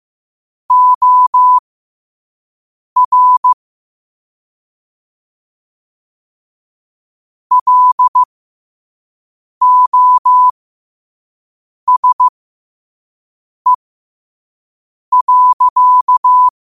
Morse code, heard as a keyed tone, about 15 wpm, spaced as for 5 wpm, 1000 Hz. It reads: OR LOSE.
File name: qotdfiles / 5wpm